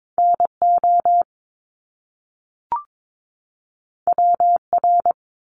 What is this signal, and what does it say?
Morse code, heard as a keyed tone, about 22 wpm, 700 Hz. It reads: NO WR
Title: Top Two Letter Combinations in Words 22wpm